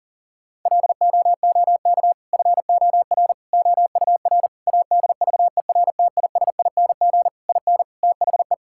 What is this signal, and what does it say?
Morse code, heard as a keyed tone, about 40 wpm, 700 Hz. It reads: LOOK FOR OUR ADVERTISING IN THE
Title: Sentences from Top 1000 Words - Review Part 6 - 40wpm